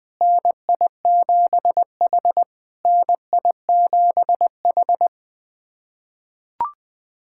Morse code, heard as a keyed tone, about 20 wpm, 700 Hz. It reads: NI7H NI7H
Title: SOTA QSO Practice 20wpm